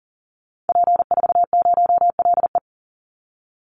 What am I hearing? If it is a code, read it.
Morse code, heard as a keyed tone, about 40 wpm, 700 Hz. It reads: P40LE
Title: Call Signs - 40wpm